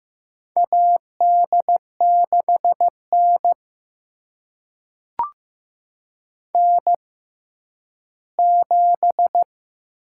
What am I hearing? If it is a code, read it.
Morse code, heard as a keyed tone, about 15 wpm, spaced as for 10 wpm, 700 Hz. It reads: AD6N N7
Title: USA Call Signs - 4 Characters - 6x Character Spacing - 15wpm